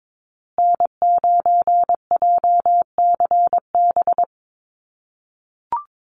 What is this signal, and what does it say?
Morse code, heard as a keyed tone, about 22 wpm, 700 Hz. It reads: N9JCB
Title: USA Call Signs Repeated 2x 22wpm